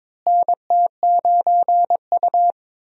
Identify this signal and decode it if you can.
Morse code, heard as a keyed tone, about 21 wpm, 700 Hz. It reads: NT9U
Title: USA Call Signs - 4 Characters - Repeated 3x - 22wpm